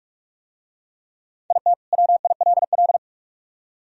Morse code, heard as a keyed tone, about 45 wpm, 700 Hz. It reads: IT WILL